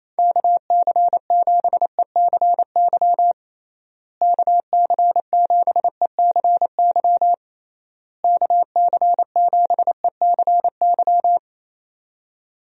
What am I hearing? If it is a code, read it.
Morse code, heard as a keyed tone, about 28 wpm, 700 Hz. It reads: KC7ECY KC7ECY KC7ECY
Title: USA Call Signs - 6 Characters - Repeated 3x - 28wpm